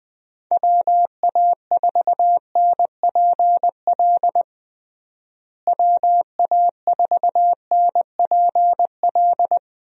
Morse code, heard as a keyed tone, about 19 wpm, 700 Hz. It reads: WA4NPL WA4NPL
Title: USA Call Signs - 6 Characters - Repeated 3x - 20wpm